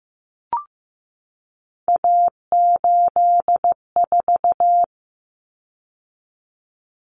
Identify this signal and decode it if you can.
Morse code, heard as a keyed tone, about 15 wpm, 700 Hz. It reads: A84
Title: Common Cut Numbers - 3-Digit 15wpm